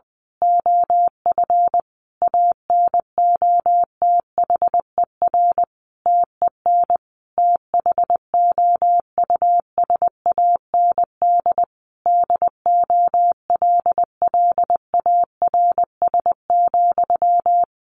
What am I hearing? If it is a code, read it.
Morse code, heard as a keyed tone, about 20 wpm, 700 Hz. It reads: OF ANOTHER TEN THOUSAND DOLLARS,
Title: A Sound Of Thunder 20wpm